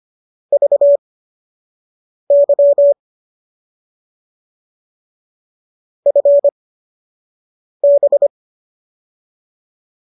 Morse code, heard as a keyed tone, about 25 wpm, spaced as for 6 wpm, 590 Hz. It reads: VY FB
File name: qso012 06